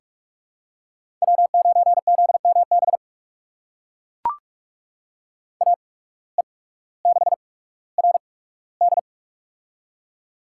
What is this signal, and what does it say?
Morse code, heard as a keyed tone, about 42 wpm, spaced as for 19 wpm, 700 Hz. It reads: W9ZMB AE6RD
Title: USA Call Signs - 5 Characters - 8x Character Spacing - 45wpm